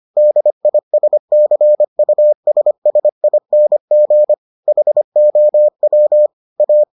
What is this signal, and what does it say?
Morse code, heard as a keyed tone, about 25 wpm, 600 Hz. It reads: DISCUSSING HOW A